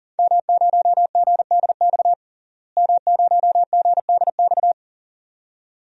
Morse code, heard as a keyed tone, about 38 wpm, 700 Hz. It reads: M0GDX M0GDX
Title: DXCC-100 Call Signs - 5 Characters - Repeated 2x - 40wpm